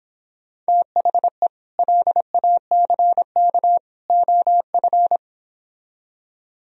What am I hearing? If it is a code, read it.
Morse code, heard as a keyed tone, about 26 wpm, 700 Hz. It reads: THE LACK OF